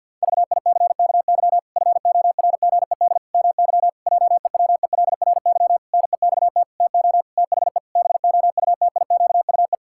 Morse code, heard as a keyed tone, about 50 wpm, 700 Hz. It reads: VICKY FORGED MY JEWELRY NEXT TO THE BOUTIQUE
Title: Pangram Challenge v2 - 50wpm